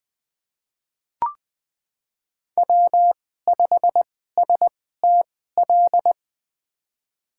Morse code, heard as a keyed tone, about 20 wpm, spaced as for 15 wpm, 700 Hz. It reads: W5STL
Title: USA Call Signs - 5 Characters - 2x Character Spacing - 20wpm